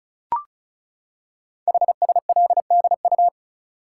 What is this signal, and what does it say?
Morse code, heard as a keyed tone, about 36 wpm, 700 Hz. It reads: HSLDU
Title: Code Groups - 5 Letter:Numbers - 35wpm